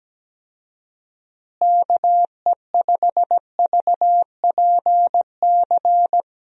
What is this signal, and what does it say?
Morse code, heard as a keyed tone, about 17 wpm, 700 Hz. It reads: KE5VPC
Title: USA Call Signs - 6 Characters - Repeated 3x - 17wpm